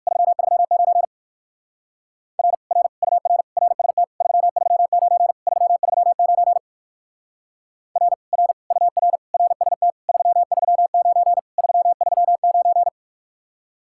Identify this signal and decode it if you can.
Morse code, heard as a keyed tone, about 53 wpm, 700 Hz. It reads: 339 R R UR RST 339 339 R R UR RST 339 339
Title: Speed-Racing Common QSO Phrases Encore - 45wpm